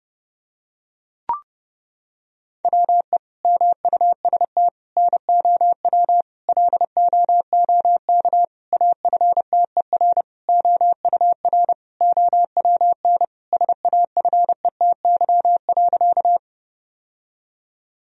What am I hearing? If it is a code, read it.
Morse code, heard as a keyed tone, about 30 wpm, 700 Hz. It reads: WE MUST NOW LOOK AFTER OUR OWN SAFETY.
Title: Sentences from Top 1000 Words - Review Part 4 - 30wpm